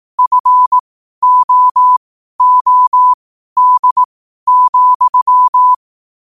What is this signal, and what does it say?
Morse code, heard as a keyed tone, about 18 wpm, spaced as for 12 wpm, 1000 Hz. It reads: FOOD,
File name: qotdfiles / 12wpm